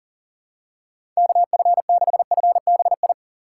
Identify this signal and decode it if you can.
Morse code, heard as a keyed tone, about 40 wpm, 700 Hz. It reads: KF6FBI
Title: USA Call Signs - 6 Characters - 6x Character Spacing - 40wpm